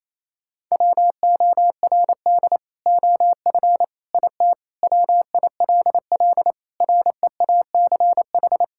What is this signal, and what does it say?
Morse code, heard as a keyed tone, about 28 wpm, 700 Hz. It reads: WORD OF IT WILL REACH
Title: Sentences from Top 1600 Words 28wpm